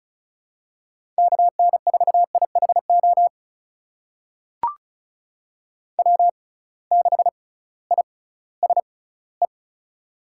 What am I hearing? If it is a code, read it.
Morse code, heard as a keyed tone, about 36 wpm, spaced as for 19 wpm, 700 Hz. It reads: KN4IHO W6ISE
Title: USA Call Signs - 6x Character Spacing - 35wpm